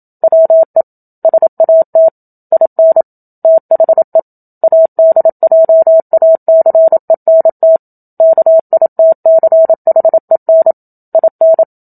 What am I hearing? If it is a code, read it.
Morse code, heard as a keyed tone, about 27 wpm, 650 Hz. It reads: WE SAT IN THE ADJACENT KITCHEN IN
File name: Book Two / 01-Chapter 1 - 27 WPM